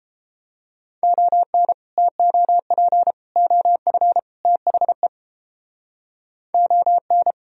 Morse code, heard as a keyed tone, about 31 wpm, 700 Hz. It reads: ON TOP OF THE ON